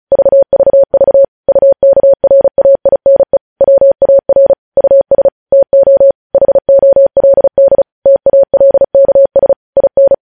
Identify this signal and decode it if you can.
Morse code, heard as a keyed tone, about 36 wpm, 550 Hz. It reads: VVV UKRAINE WAR US TO HOLD TALKS IN
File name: podcast / News headlines-1x-35x35